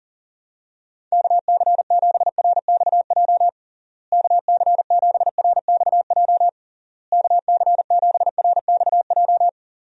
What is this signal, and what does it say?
Morse code, heard as a keyed tone, about 40 wpm, 700 Hz. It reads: KC7RXJ KC7RXJ KC7RXJ
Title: USA Call Signs - 6 Characters - Repeated 3x - 40wpm